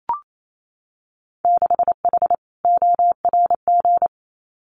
Morse code, heard as a keyed tone, about 28 wpm, 700 Hz. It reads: 6H ORG
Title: Winter Field Day Exchanges Encore 28wpm